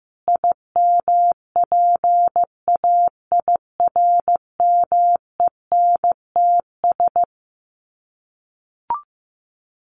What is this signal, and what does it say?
Morse code, heard as a keyed tone, about 15 wpm, 700 Hz. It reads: IMPAIRMENTS
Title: Sets of 5 Words Encore 15wpm